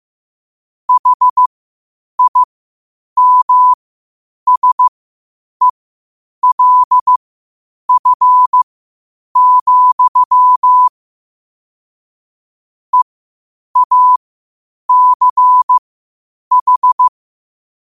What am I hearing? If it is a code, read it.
Morse code, heard as a keyed tone, about 15 wpm, spaced as for 8 wpm, 1000 Hz. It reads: HIMSELF, EACH
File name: qotdfiles / 8wpm